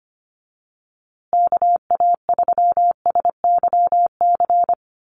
Morse code, heard as a keyed tone, about 25 wpm, 700 Hz. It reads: KA3SYC
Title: USA Call Signs Repeated 2x 25wpm